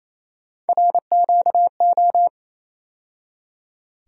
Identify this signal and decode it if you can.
Morse code, heard as a keyed tone, about 27 wpm, 700 Hz. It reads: RQO
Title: USA Call Sign Postfixes - 3 Letters 28wpm